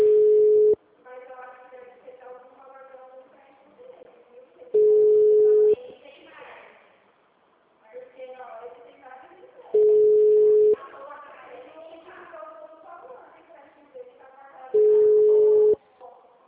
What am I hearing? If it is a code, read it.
Ringback tone (Europe, 425 Hz), with speech.
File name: out- -2014- - - .34644